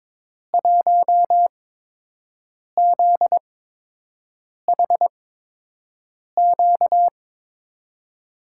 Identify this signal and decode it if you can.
Morse code, heard as a keyed tone, about 22 wpm, spaced as for 6 wpm, 700 Hz. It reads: 1ZHQ
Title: USA Call Signs - 6 Characters - 8x Character Spacing - 22wpm